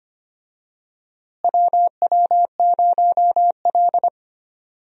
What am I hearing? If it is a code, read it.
Morse code, heard as a keyed tone, about 25 wpm, 700 Hz. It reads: WW0L
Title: USA Call Signs - 4 Characters - Repeated 2x - 25wpm